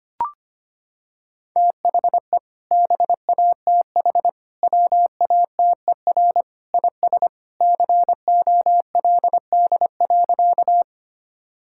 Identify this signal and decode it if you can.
Morse code, heard as a keyed tone, about 25 wpm, 700 Hz. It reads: THE BATH WATER IS COLD.
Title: Sentences from Top 1600 Words 25wpm